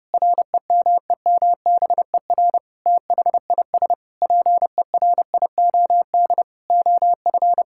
Morse code, heard as a keyed tone, about 30 wpm, 700 Hz. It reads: REMEMBER THIS PERIOD OF
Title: Sentences from Top 1000 Words - Review Part 5 - 30wpm